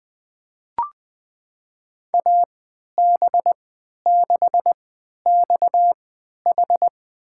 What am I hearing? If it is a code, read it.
Morse code, heard as a keyed tone, about 20 wpm, spaced as for 11 wpm, 700 Hz. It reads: AB6XH
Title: USA Call Signs - 5 Characters - 3x Character Spacing - 20wpm